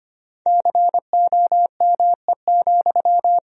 Morse code, heard as a keyed tone, about 25 wpm, 700 Hz. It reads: COME,
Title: Sentences from Top 1000 Words - Review Part 4 - 25wpm